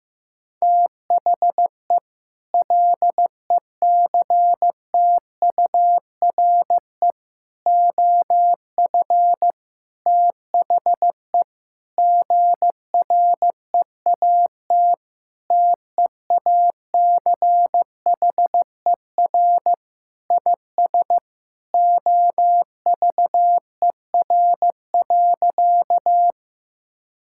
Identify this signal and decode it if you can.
Morse code, heard as a keyed tone, about 15 wpm, 700 Hz. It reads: THE LECTURE OF THE GREAT TEACHER IS OVER.
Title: Sentences from Top 1600 Words 15wpm